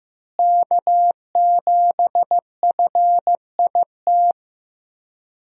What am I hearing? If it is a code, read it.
Morse code, heard as a keyed tone, about 15 wpm, 700 Hz. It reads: K7FIT
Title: USA Call Signs - 5 Characters - Repeated 3x - 15wpm